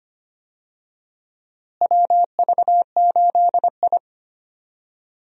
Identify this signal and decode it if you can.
Morse code, heard as a keyed tone, about 25 wpm, 700 Hz. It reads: WV8I
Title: USA Call Signs - 4 Characters - Repeated 3x - 25wpm